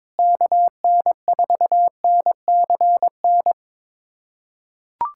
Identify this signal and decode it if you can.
Morse code, heard as a keyed tone, about 22 wpm, 700 Hz. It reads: KN4NCN
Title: USA Callsign DE Callsign 22wpm